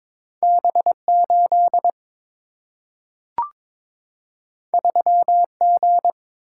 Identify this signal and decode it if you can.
Morse code, heard as a keyed tone, about 22 wpm, 700 Hz. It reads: B8 3G
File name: practice / 22wpm / Two Letter-Number Combinations - 22wpm